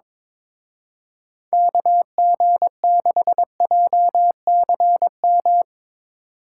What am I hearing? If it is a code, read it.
Morse code, heard as a keyed tone, about 24 wpm, 700 Hz. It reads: KG6JCM
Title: Speed Racing - USA 2x3 Call Signs 22wpm